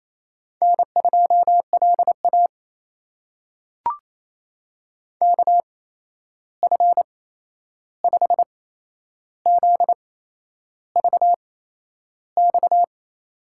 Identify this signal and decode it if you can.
Morse code, heard as a keyed tone, about 28 wpm, spaced as for 11 wpm, 700 Hz. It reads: N2LA KF5ZVX
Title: USA Call Signs - 8x Character Spacing - 28wpm